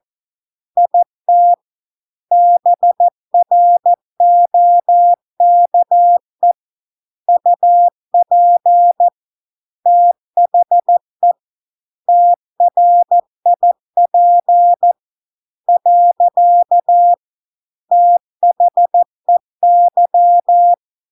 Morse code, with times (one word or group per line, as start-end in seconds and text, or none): 0.8-1.5 IT
2.3-6.5 BROKE
7.3-9.1 UP
9.9-11.3 THE
12.1-14.9 TRIP
15.7-17.1 .
17.9-20.7 THEY